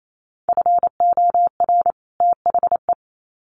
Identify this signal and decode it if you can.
Morse code, heard as a keyed tone, about 28 wpm, 700 Hz. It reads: FOR THE
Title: Sets of 2 Words 28wpm